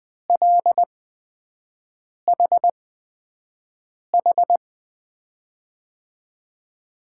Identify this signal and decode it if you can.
Morse code, heard as a keyed tone, about 20 wpm, spaced as for 5 wpm, 700 Hz. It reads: LHH
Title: USA Call Signs - 6 Characters - 8x Character Spacing - 20wpm